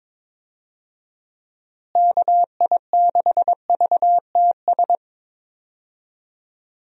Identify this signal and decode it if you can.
Morse code, heard as a keyed tone, about 22 wpm, 700 Hz. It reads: KI6VTS
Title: USA Call Signs - 6 Characters - Repeated 2x - 22wpm